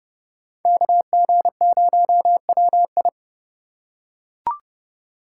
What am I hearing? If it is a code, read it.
Morse code, heard as a keyed tone, about 28 wpm, 700 Hz. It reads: KG0WI